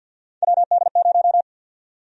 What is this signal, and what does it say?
Morse code, heard as a keyed tone, about 51 wpm, 700 Hz. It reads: WD0
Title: USA Call Sign Prefixes - 50wpm